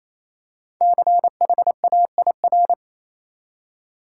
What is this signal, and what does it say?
Morse code, heard as a keyed tone, about 27 wpm, 700 Hz. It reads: CHAIR
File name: Easy Words - Speed-Racing with Hints - 22wpm